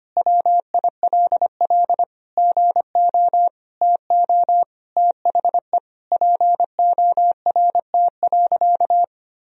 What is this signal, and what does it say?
Morse code, heard as a keyed tone, about 25 wpm, 700 Hz. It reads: WILL GO TO THE PORT.
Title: Sentences from Top 1000 Words - Review Part 1 - 25wpm